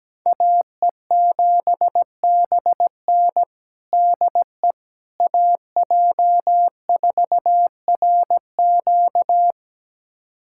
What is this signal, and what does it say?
Morse code, heard as a keyed tone, about 17 wpm, 700 Hz. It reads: AE7BN DE AJ4RQ
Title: USA Callsign DE Callsign - 5 Characters - 17wpm